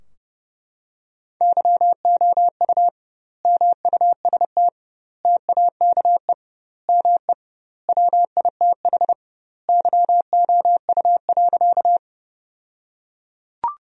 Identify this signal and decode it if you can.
Morse code, heard as a keyed tone, about 30 wpm, 700 Hz. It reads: YOU MUST TAKE ME WITH YOU.
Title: 2x Word-Spacing - Sentences from Top 200 Words - 30wpm